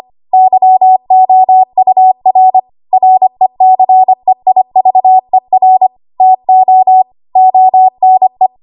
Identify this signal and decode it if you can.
Morse code, heard as a keyed tone, about 25 wpm, 750 Hz. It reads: YOUR RECEIVER TO ONE
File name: Archive / 25 WPM / 25WPM